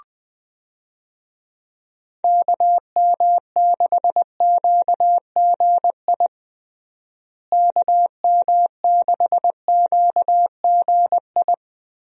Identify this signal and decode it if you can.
Morse code, heard as a keyed tone, about 20 wpm, 700 Hz. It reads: KM6QGI KM6QGI
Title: USA Call Signs - 6 Characters - Repeated 2x - 20wpm